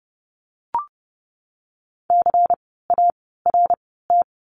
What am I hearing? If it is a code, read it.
Morse code, heard as a keyed tone, about 30 wpm, 700 Hz. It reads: C A R T
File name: Lesson 16f - Letter c - Words 30wpm